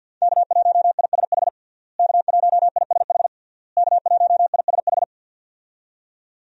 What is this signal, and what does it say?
Morse code, heard as a keyed tone, about 49 wpm, 700 Hz. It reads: K1ISH K1ISH K1ISH
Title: USA Call Signs - 5 Characters - Repeated 3x - 50wpm